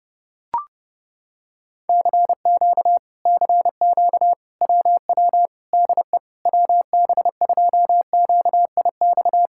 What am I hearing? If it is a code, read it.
Morse code, heard as a keyed tone, about 31 wpm, 700 Hz. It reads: CQ CQ WW DE WB2QIX